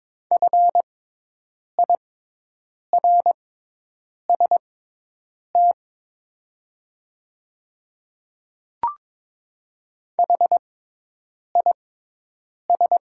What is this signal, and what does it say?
Morse code, heard as a keyed tone, about 22 wpm, spaced as for 8 wpm, 690 Hz. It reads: FIRST HIS
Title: 6x Character Spacing - Top 100 Words - 22wpm